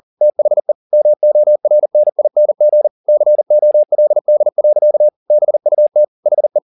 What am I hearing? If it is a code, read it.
Morse code, heard as a keyed tone, about 40 wpm, 600 Hz. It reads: THE MORNING COLD. BUT HE